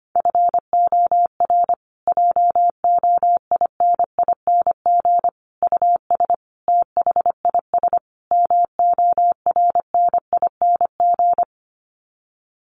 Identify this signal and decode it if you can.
Morse code, heard as a keyed tone, about 25 wpm, 700 Hz. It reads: FOR JOINING US THIS MORNING
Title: Sets of 5 Words Encore 25wpm